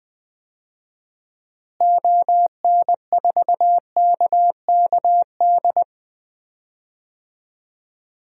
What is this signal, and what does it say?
Morse code, heard as a keyed tone, about 21 wpm, 700 Hz. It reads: ON4KKD
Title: DXCC-100 Call Signs - 6 Characters - Repeated 2x - 20wpm